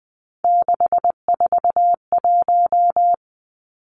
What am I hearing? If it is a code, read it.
Morse code, heard as a keyed tone, about 20 wpm, 700 Hz. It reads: 641
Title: Three Challenging Letter-Number Combinations 20wpm